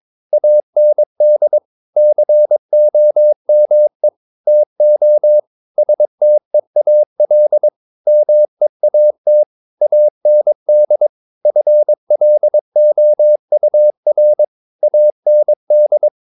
Morse code, heard as a keyed tone, about 23 wpm, 600 Hz. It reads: AND COME TO STEAL MEAT AND FLOUR AND
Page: Romanen Never af Ken Follet på engelsk.